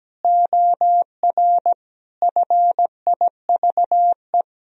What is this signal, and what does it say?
Morse code, heard as a keyed tone, about 17 wpm, 700 Hz. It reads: OR FIVE
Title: Sentences from Top 1000 Words - Review Part 2 - 17wpm